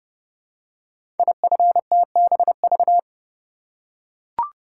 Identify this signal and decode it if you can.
Morse code, heard as a keyed tone, about 30 wpm, 700 Hz. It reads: IFTBV
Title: Code Groups - 5 Letters - 30wpm